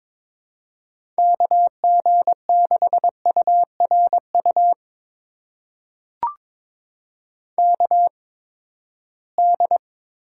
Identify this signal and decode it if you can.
Morse code, heard as a keyed tone, about 22 wpm, spaced as for 15 wpm, 700 Hz. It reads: KG6URU KD
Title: USA Call Signs - 6 Characters - 8x Character Spacing - 22wpm